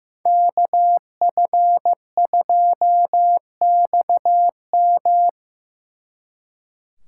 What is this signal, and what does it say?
Morse code, heard as a keyed tone, about 18 wpm, 700 Hz. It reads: KF2XM
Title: Speed Racing - USA 2x2 Call Signs 15wpm